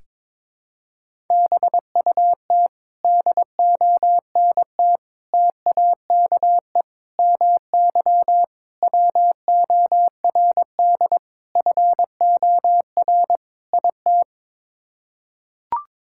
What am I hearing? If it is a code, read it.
Morse code, heard as a keyed tone, about 22 wpm, 700 Hz. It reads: BUT DONT TAKE MY WORD FOR IT
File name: practice / 22wpm / Sets of 7 Words - 22wpm